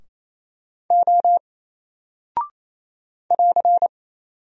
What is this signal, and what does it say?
Morse code, heard as a keyed tone, about 27 wpm, 700 Hz. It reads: O <AR>
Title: Single Character (Letter:Number:Punctuation:Prosign) 28wpm